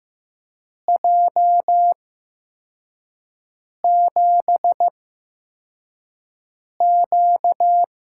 Morse code, heard as a keyed tone, about 15 wpm, spaced as for 4 wpm, 700 Hz. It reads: J7Q
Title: USA Call Signs - 4 Characters - 8x Character Spacing - 15wpm